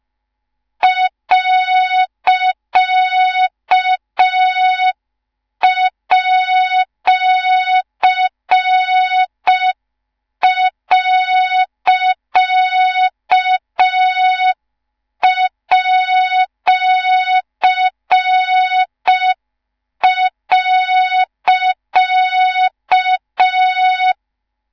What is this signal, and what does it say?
Morse code, heard as a keyed tone, about 5 wpm, 3700 Hz, with noise. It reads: .@.@.